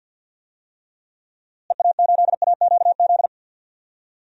Morse code, heard as a keyed tone, about 51 wpm, 700 Hz. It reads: EA8AQZ
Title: DXCC-100 Call Signs - 6 Characters - Repeated 2x - 50wpm